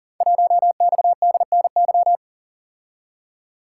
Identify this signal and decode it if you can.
Morse code, heard as a keyed tone, about 39 wpm, 700 Hz. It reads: 1XDNY
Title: Code Groups - 5 Letter:Numbers - 40wpm